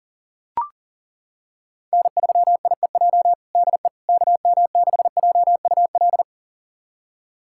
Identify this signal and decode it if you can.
Morse code, heard as a keyed tone, about 37 wpm, 700 Hz. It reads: N3IEJ DE KM6JUL
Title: USA Callsign DE Callsign 40wpm